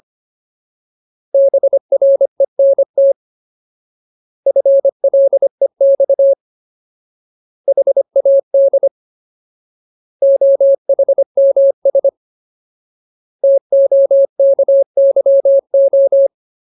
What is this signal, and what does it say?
Morse code, heard as a keyed tone, about 25 wpm, 550 Hz. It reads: BRENT FLEX HAD OHMS TOKYO
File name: Ultimate Word List / 25WPM Ultimate